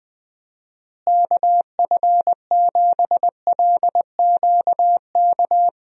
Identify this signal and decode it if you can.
Morse code, heard as a keyed tone, about 20 wpm, 700 Hz. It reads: KF7LQK
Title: USA Call Signs - 6 Characters - Repeated 3x - 20wpm